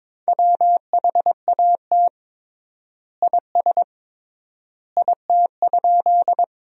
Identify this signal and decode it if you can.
Morse code, heard as a keyed tone, about 22 wpm, 700 Hz. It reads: WHAT IS IT?
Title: 3x Word-Spacing - Sentences from Top 100 Words - 22wpm